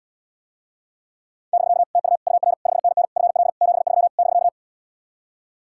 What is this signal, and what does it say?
Morse code, heard as a keyed tone, about 80 wpm, 700 Hz. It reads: <BK> TU UR 5NN FL 73 <BK>